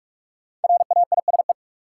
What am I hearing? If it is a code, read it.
Morse code, heard as a keyed tone, about 46 wpm, 700 Hz. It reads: RAISE
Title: Sentences from Top 1400 Words - New Word Drill Alt - 45wpm